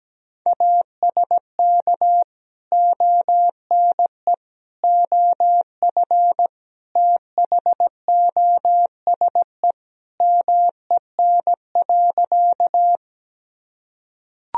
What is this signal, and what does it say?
Morse code, heard as a keyed tone, about 17 wpm, 700 Hz. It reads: ASK ONE OF THOSE MEN.
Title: Sentences from Top 1000 Words - Review Part 1 - 17wpm